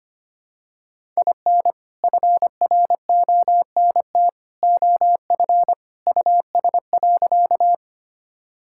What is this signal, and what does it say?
Morse code, heard as a keyed tone, about 25 wpm, 700 Hz. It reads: IN FRONT OF US.
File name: practice / 25wpm / Sentences with 2-Word Phrases - 25wpm